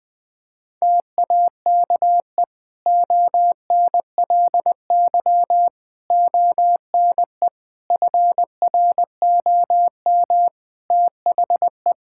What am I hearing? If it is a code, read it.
Morse code, heard as a keyed tone, about 21 wpm, 700 Hz. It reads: TAKE ONLY ONE FROM THE
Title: Sentences from Top 1000 Words - Review Part 5 - 20wpm